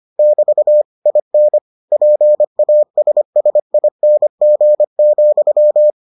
Morse code, heard as a keyed tone, about 25 wpm, 600 Hz. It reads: <BT> IN PASSING,